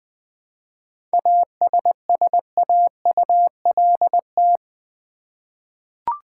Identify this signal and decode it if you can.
Morse code, heard as a keyed tone, about 20 wpm, 700 Hz. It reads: ASSAULT
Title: 7 Letter Words - First 3 Letters Given - 20wpm